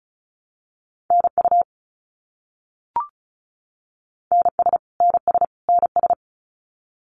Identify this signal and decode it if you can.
Morse code, heard as a keyed tone, about 36 wpm, 700 Hz. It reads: NU NS NS NS
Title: Canadian Provinces and Territories - Repeated 3x - 35wpm